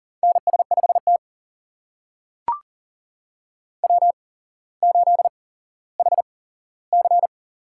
Morse code, heard as a keyed tone, about 40 wpm, spaced as for 17 wpm, 700 Hz. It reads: NS5T W8HC
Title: USA Call Signs - 4 Characters - 8x Character Spacing - 40wpm